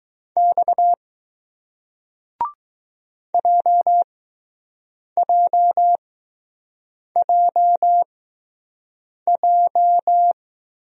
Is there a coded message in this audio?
Yes: Morse code from 0.4 to 10.3 s, heard as a keyed tone, about 20 wpm, spaced as for 6 wpm, 700 Hz. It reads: X JJJJ